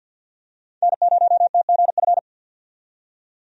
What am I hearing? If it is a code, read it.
Morse code, heard as a keyed tone, about 53 wpm, 700 Hz. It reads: N0TGF
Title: USA Call Signs - 5 Characters - 50wpm